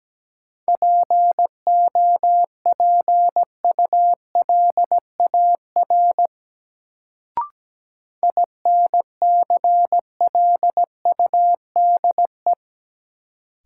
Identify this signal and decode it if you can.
Morse code, heard as a keyed tone, about 17 wpm, 700 Hz. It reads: POPULAR INCLUDE
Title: Sentences from Top 1000 Words - Review Part 3 - New Word Drill - 17wpm